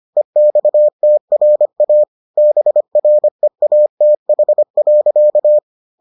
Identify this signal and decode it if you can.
Morse code, heard as a keyed tone, about 25 wpm, 600 Hz. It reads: EXTRA BREATH.